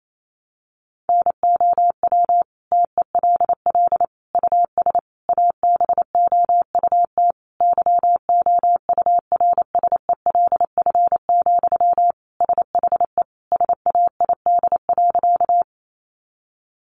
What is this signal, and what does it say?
Morse code, heard as a keyed tone, about 28 wpm, 700 Hz. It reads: NOW TELL US ABOUT YOURSELF, SHE SAID.
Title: Sentences from Top 600 Words 28wpm